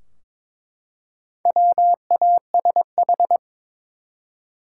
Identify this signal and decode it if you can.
Morse code, heard as a keyed tone, about 22 wpm, 700 Hz. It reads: WASH
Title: Four Letter Words - 22wpm